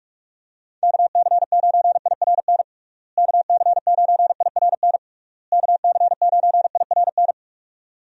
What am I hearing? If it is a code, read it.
Morse code, heard as a keyed tone, about 46 wpm, 700 Hz. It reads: KC9IRN KC9IRN KC9IRN
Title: USA Call Signs - 6 Characters - Repeated 3x - 45wpm